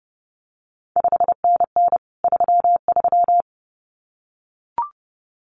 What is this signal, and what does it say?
Morse code, heard as a keyed tone, about 30 wpm, 700 Hz. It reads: 5NN 33
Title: CQ WW DX Contest Exchange Encore - 30wpm